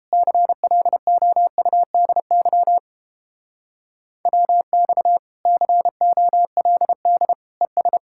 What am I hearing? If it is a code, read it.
Morse code, heard as a keyed tone, about 31 wpm, 700 Hz. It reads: CLOUDY WX COLD ES